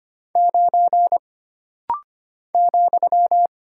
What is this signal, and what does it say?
Morse code, heard as a keyed tone, about 25 wpm, 700 Hz. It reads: 9 ,
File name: Single Character (Letter:Number:Punctuation:Prosign) - Rapid-Fire - 25wpm